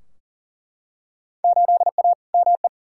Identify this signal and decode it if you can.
Morse code, heard as a keyed tone, about 40 wpm, 700 Hz. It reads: 8A ME
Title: Field Day Exchanges - 40wpm